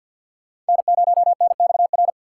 Morse code, heard as a keyed tone, about 46 wpm, 700 Hz. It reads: N0NXR
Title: USA Callsign DE Callsign - 5 Characters - 50wpm